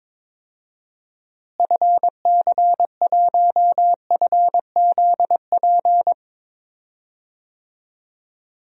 Morse code, heard as a keyed tone, about 22 wpm, 700 Hz. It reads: FC1FZP
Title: DXCC-100 Call Signs - 6 Characters - Repeated 3x - 22wpm